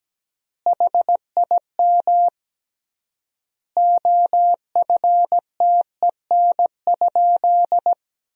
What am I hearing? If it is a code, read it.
Morse code, heard as a keyed tone, about 17 wpm, 700 Hz. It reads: HIM OFTEN?
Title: 3x Word-Spacing - Sentences from Top 400 Words - 17wpm